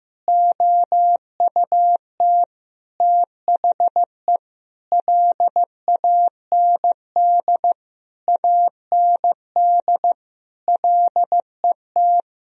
Morse code, heard as a keyed tone, about 15 wpm, 700 Hz. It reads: OUT THE LAND AND LET
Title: Sentences from Top 1000 Words - Review Part 2 - 15wpm